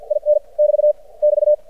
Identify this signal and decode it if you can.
Morse code, heard as a keyed tone, about 52 wpm, spaced as for 44 wpm, 600 Hz. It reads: IN <BT> <BT>
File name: archived / CRSnet